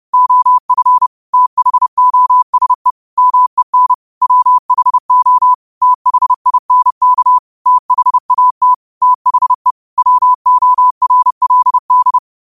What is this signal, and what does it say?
Morse code, heard as a keyed tone, about 30 wpm, 1000 Hz. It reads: OF THOSE MEN WHO THINK THAT THE WORLD